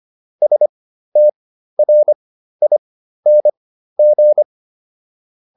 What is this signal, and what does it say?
Morse code, heard as a keyed tone, about 25 wpm, spaced as for 13 wpm, 600 Hz. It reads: STRING